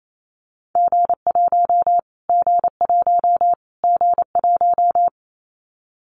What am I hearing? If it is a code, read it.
Morse code, heard as a keyed tone, about 28 wpm, 700 Hz. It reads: G1 G1 G1
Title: Callsign Prefixes for Top 25 DXCC - Repeated 3x - 28wpm